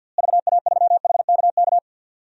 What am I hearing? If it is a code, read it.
Morse code, heard as a keyed tone, about 47 wpm, 700 Hz. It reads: VA3HKK
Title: DXCC-100 Call Signs - 6 Characters - Repeated 2x - 50wpm